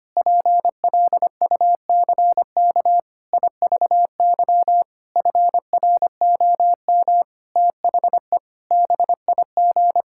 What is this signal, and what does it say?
Morse code, heard as a keyed tone, about 25 wpm, 700 Hz. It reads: PLUCK IVY FROM THE BIG
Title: Pangram Challenge v2 - 25wpm